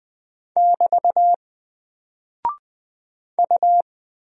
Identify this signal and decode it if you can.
Morse code, heard as a keyed tone, about 20 wpm, 700 Hz. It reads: <BT> U
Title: Single Character (Letter:Number:Punctuation:Prosign) Alt 20wpm